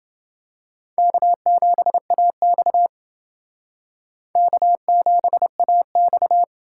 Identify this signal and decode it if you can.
Morse code, heard as a keyed tone, about 28 wpm, 700 Hz. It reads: K7AX K7AX